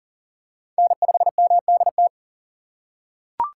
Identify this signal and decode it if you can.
Morse code, heard as a keyed tone, about 38 wpm, 700 Hz. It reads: N5MDT
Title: USA Call Signs - 5 Characters - 4x Character Spacing - 40wpm